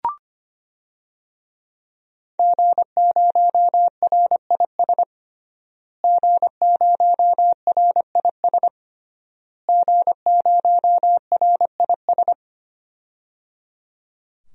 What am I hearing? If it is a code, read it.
Morse code, heard as a keyed tone, about 25 wpm, 700 Hz. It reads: G0RIS G0RIS G0RIS
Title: DXCC-100 Call Signs - 5 Characters - Repeated 3x - 25wpm